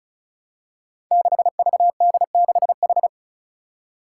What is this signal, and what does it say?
Morse code, heard as a keyed tone, about 36 wpm, 700 Hz. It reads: 6VD6H
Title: Five Challenging Letter-Number Combinations 35wpm